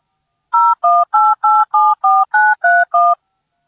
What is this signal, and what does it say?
DTMF, heard as telephone touch tones, tones of about 220 ms. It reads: 018874931